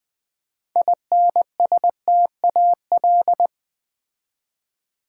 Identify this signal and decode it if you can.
Morse code, heard as a keyed tone, about 20 wpm, 700 Hz. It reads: INSTAL
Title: Build A Word - 6 Letters At A Time - 12-Letter Words - 20wpm